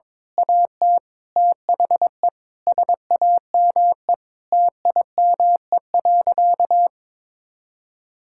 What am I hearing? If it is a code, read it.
Morse code, heard as a keyed tone, about 21 wpm, 700 Hz. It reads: AT THE SAME TIME.
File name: Difficult Sentences with Hints - 22wpm